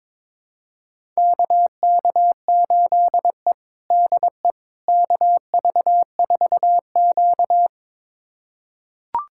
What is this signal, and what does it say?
Morse code, heard as a keyed tone, about 22 wpm, 700 Hz. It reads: KK8E DE KV4Q
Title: USA Callsign DE Callsign - 4 Characters - 22wpm